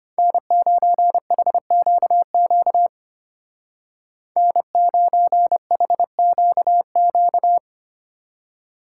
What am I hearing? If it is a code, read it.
Morse code, heard as a keyed tone, about 32 wpm, 700 Hz. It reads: N9HQQ N9HQQ